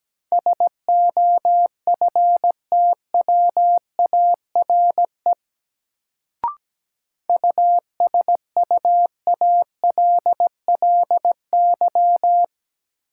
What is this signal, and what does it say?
Morse code, heard as a keyed tone, about 17 wpm, 700 Hz. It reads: SOFTWARE USUALLY
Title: Sentences from Top 1000 Words - Review Part 3 - New Word Drill - 17wpm